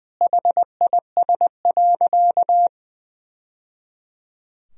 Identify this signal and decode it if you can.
Morse code, heard as a keyed tone, about 21 wpm, 700 Hz. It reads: HIS.
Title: Sentences from Top 1000 Words - Review Part 3 - 20wpm